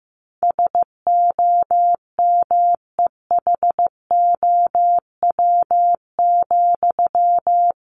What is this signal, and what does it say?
Morse code, heard as a keyed tone, about 15 wpm, 700 Hz. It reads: SOMEHOW,
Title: A Sound Of Thunder 15wpm